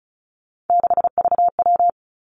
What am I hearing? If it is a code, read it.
Morse code, heard as a keyed tone, about 36 wpm, 700 Hz. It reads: 6VW
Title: Three Challenging Letter-Number Combinations 35wpm